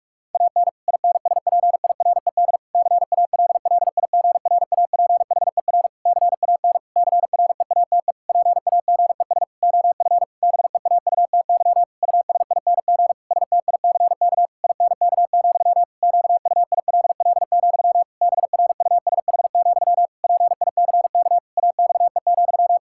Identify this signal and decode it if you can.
Morse code, heard as a keyed tone, about 41 wpm, 700 Hz. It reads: AN INSPIRED CALLIGRAPHER CAN CREATE PAGES OF BEAUTY USING STICK INK, QUILL, BRUSH, PICK AXE,
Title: Pangram Challenge v2 - 45wpm